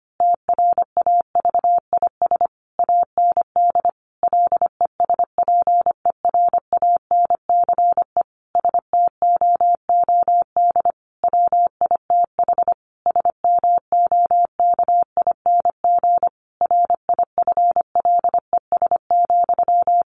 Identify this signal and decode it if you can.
Morse code, heard as a keyed tone, about 25 wpm, 700 Hz. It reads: TRAVIS AND LESPERANCE STOOD WITH SMOKING RIFLES,
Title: A Sound Of Thunder 25wpm